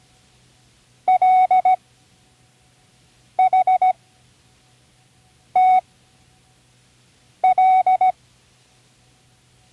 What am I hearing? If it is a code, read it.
Morse code, heard as a keyed tone, about 17 wpm, spaced as for 5 wpm, 750 Hz. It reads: LHTL
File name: CWResources / Learning Morse Code (from Tune in the World) / 24 random